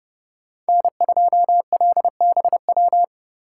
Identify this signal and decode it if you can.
Morse code, heard as a keyed tone, about 29 wpm, 700 Hz. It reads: N2LBW